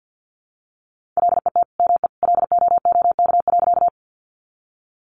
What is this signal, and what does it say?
Morse code, heard as a keyed tone, about 50 wpm, 700 Hz. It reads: LET ME LOOK.
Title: Sentences from Top 300 Words 50wpm